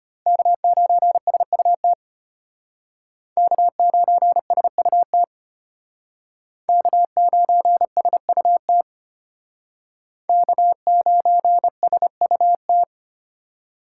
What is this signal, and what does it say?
Morse code, heard as a keyed tone, about 32 wpm, 700 Hz. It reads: K9SUT K9SUT K9SUT K9SUT